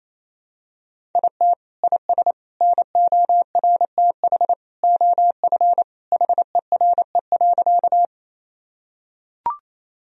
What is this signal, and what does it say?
Morse code, heard as a keyed tone, about 28 wpm, 700 Hz. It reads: IT IS NORTH OF HERE.
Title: Sentences from Top 1000 Words - Review Part 1 - 28wpm